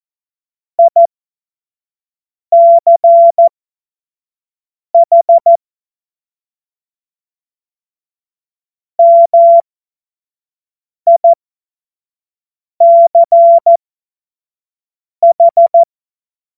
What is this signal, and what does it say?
Morse code, heard as a keyed tone, about 14 wpm, spaced as for 5 wpm, 680 Hz. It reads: ICH MICH
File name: Morsekurs / Witze / Witz 19 März 2024 25 F70